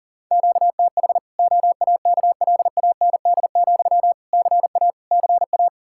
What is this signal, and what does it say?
Morse code, heard as a keyed tone, about 40 wpm, 700 Hz. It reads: QTH OAKLAND, CA CA
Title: Practice QSO - Exchange 1 - Beginning - 40wpm